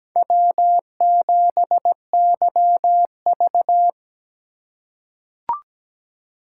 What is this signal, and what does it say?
Morse code, heard as a keyed tone, about 17 wpm, 700 Hz. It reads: W7YV
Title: USA Call Signs - 4 Characters - 3x Character Spacing - 17wpm